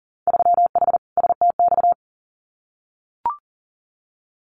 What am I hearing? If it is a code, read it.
Morse code, heard as a keyed tone, about 38 wpm, 700 Hz. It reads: 3H STX
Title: Winter Field Day Exchanges 40wpm